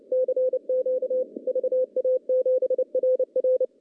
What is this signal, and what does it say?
Morse code, heard as a keyed tone, about 30 wpm, 500 Hz. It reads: CQ VA7RR
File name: contests / CW-SS-k5tr- -0900